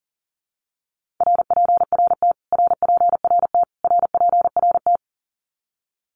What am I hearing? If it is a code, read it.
Morse code, heard as a keyed tone, about 37 wpm, 700 Hz. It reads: RPRT RPRT RPRT
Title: Abbreviations Repeated 3x - 40wpm